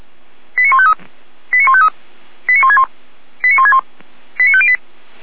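Five-tone selective call (ZVEI), heard as quick runs of five tones, tones of about 70 ms. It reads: 89153 89153 89161 89161 89508
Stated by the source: ZVEI five-tone call 89099 received by SDR.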